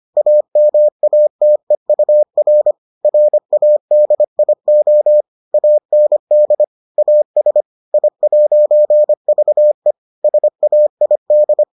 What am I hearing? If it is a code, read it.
Morse code, heard as a keyed tone, about 25 wpm, 600 Hz. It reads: AMATEUR RADIO AND AS I'VE SAID